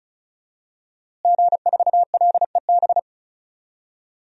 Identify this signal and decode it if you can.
Morse code, heard as a keyed tone, about 36 wpm, 700 Hz. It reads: G4LEB
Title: DXCC-100 Call Signs - 5 Characters - Repeated 3x - 35wpm